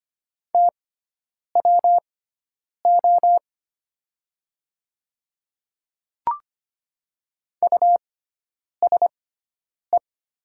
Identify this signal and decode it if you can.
Morse code, heard as a keyed tone, about 25 wpm, spaced as for 9 wpm, 690 Hz. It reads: TWO USE
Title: 6x Character Spacing - Top 500 Words - 25wpm